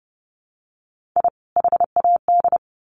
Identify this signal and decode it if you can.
Morse code, heard as a keyed tone, about 30 wpm, 700 Hz. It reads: I HAD
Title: Sets of 2 Words 30wpm